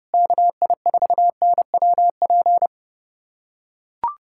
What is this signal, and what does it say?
Morse code, heard as a keyed tone, about 32 wpm, 700 Hz. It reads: KI4NWP
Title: USA Callsign DE Callsign 30wpm